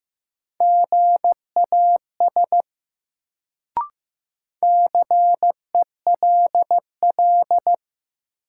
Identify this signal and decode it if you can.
Morse code, heard as a keyed tone, about 15 wpm, 700 Hz. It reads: GAS CELL
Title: Sentences from Top 1100 Words - New Word Drill - 15wpm